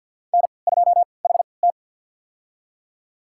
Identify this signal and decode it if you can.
Morse code, heard as a keyed tone, about 53 wpm, spaced as for 29 wpm, 700 Hz. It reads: N2HT
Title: USA Call Signs - 4 Characters - 3x Character Spacing - 50wpm